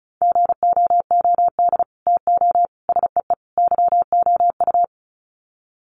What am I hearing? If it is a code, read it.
Morse code, heard as a keyed tone, about 34 wpm, 700 Hz. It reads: GOOD TO SEE YOU
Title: Sets of 4 Words 35wpm